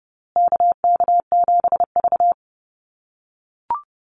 Morse code, heard as a keyed tone, about 30 wpm, 700 Hz. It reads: KK7V
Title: USA Call Signs Repeated 2x 30wpm